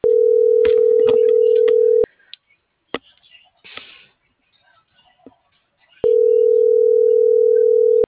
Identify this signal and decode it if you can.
Ringback tone (North America), 440 + 480 Hz.
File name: CENTRAL MAYAB